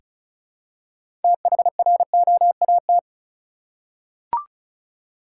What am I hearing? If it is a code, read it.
Morse code, heard as a keyed tone, about 36 wpm, 700 Hz. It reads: THROAT
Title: Six Letter Words - 35wpm